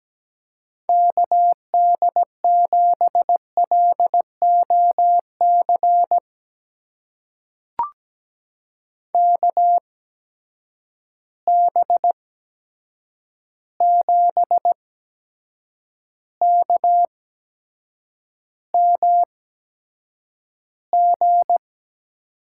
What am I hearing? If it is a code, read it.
Morse code, heard as a keyed tone, about 17 wpm, spaced as for 7 wpm, 700 Hz. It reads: KD7LOC KB7KMG
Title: USA Call Signs - 6 Characters - 8x Character Spacing - 17wpm